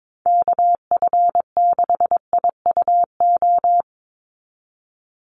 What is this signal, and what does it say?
Morse code, heard as a keyed tone, about 22 wpm, 700 Hz. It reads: KF6IUO
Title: USA Call Signs 22wpm